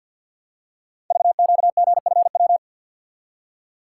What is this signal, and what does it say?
Morse code, heard as a keyed tone, about 46 wpm, 700 Hz. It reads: VQGFW
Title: Code Groups - 5 Letters - 50wpm